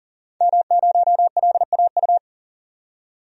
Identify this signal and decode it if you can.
Morse code, heard as a keyed tone, about 40 wpm, 700 Hz. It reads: M0LAU
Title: DXCC-100 Call Signs - 5 Characters - 40wpm